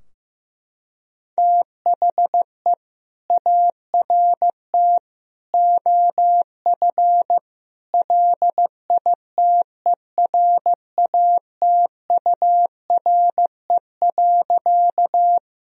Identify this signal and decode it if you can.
Morse code, heard as a keyed tone, about 15 wpm, 700 Hz. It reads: THE ART OF LITERATURE.
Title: Sentences from Top 1000 Words - Review Part 3 - 15wpm